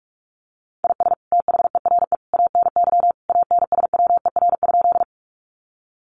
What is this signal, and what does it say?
Morse code, heard as a keyed tone, about 44 wpm, 700 Hz. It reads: IS THERE ANY ANSWER?
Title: Sentences from Top 300 Words 45wpm